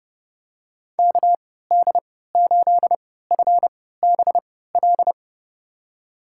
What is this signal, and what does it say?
Morse code, heard as a keyed tone, about 30 wpm, spaced as for 17 wpm, 700 Hz. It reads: KD8FBL
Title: USA Call Signs - 3x Character Spacing - 30wpm